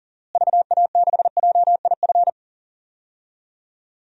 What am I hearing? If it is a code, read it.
Morse code, heard as a keyed tone, about 40 wpm, 700 Hz. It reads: VA6JIF
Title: DXCC-100 Call Signs - 6 Characters - Repeated 3x - 40wpm